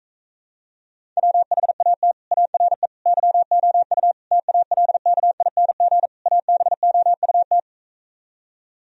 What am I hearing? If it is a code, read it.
Morse code, heard as a keyed tone, about 45 wpm, 700 Hz. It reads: WHAT ARE YOU TALKING ABOUT